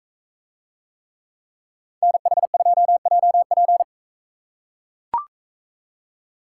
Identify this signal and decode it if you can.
Morse code, heard as a keyed tone, about 42 wpm, 700 Hz. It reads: NH2JP